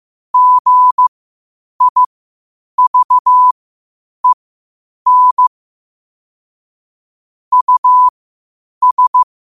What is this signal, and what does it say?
Morse code, heard as a keyed tone, about 15 wpm, spaced as for 8 wpm, 1000 Hz. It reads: GIVEN US
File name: qotdfiles / 8wpm